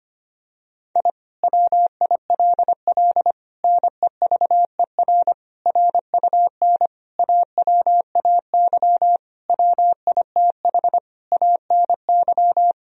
Morse code, heard as a keyed tone, about 25 wpm, 700 Hz. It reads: I WILL NEVER RUN AWAY WITH ANY
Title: Sentences from Top 1000 Words - Review Part 1 - 25wpm